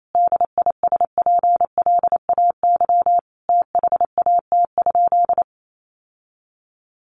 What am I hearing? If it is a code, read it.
Morse code, heard as a keyed tone, about 28 wpm, 700 Hz. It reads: DISPLAY THAT?
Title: Sentences from Top 1200 Words 28wpm